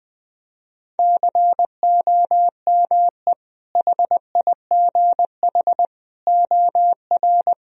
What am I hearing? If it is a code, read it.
Morse code, heard as a keyed tone, about 21 wpm, 700 Hz. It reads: COME HIGH OR
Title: Sentences from Top 1000 Words - Review Part 1 - 20wpm